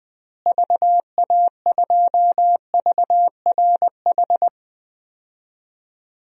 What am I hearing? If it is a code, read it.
Morse code, heard as a keyed tone, about 21 wpm, 700 Hz. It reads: VA2VRH
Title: DXCC-100 Call Signs - 6 Characters - Repeated 3x - 20wpm